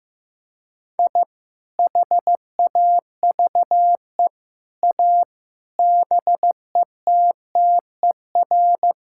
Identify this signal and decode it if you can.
Morse code, heard as a keyed tone, about 16 wpm, 700 Hz. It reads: I HAVE A BETTER
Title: Sentences from Top 1000 Words - Review Part 5 - 15wpm